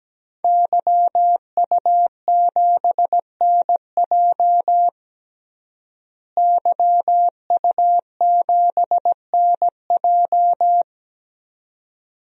Morse code, heard as a keyed tone, about 17 wpm, 700 Hz. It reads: YU7NJ YU7NJ
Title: DXCC-100 Call Signs - 5 Characters - Repeated 2x - 17wpm